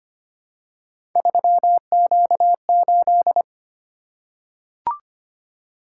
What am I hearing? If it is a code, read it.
Morse code, heard as a keyed tone, about 25 wpm, 700 Hz. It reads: 3Q8
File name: Three Letter-Number Combinations - 25wpm